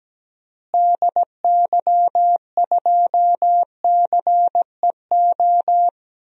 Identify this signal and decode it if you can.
Morse code, heard as a keyed tone, about 17 wpm, 700 Hz. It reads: DY2CEO
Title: DXCC-100 Call Signs - 6 Characters - Repeated 3x - 17wpm